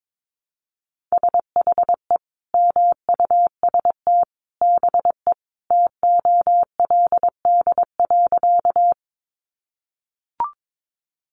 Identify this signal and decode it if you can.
Morse code, heard as a keyed tone, about 22 wpm, 700 Hz. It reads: SHE MUST BE TOLD.
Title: Sentences from Top 400 Words 22wpm